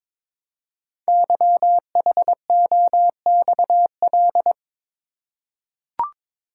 Morse code, heard as a keyed tone, about 22 wpm, 700 Hz. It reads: YHOXL
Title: Code Groups - 5 Letters - 22wpm